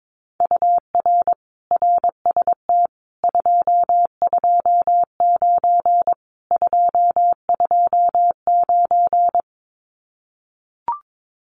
Morse code, heard as a keyed tone, about 22 wpm, 700 Hz. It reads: UR RST 229 229
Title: Common QSO Phrases Encore 22wpm